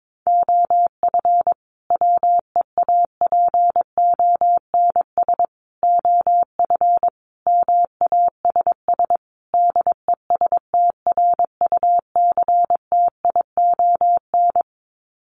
Morse code, heard as a keyed tone, about 22 wpm, 700 Hz. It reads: OF WEAPONS OF MASS DESTRUCTION
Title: Sets of 5 Words Encore 22wpm